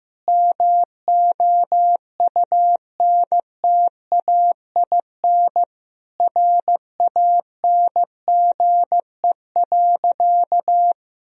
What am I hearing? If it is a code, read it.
Morse code, heard as a keyed tone, about 16 wpm, 700 Hz. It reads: MOUNTAIN RANGE.
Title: Sentences from Top 1000 Words - Review Part 4 - 15wpm